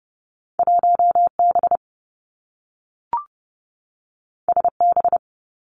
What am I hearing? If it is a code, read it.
Morse code, heard as a keyed tone, about 30 wpm, 700 Hz. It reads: 1 B S B
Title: Two Challenging Letter-Number Combinations 30wpm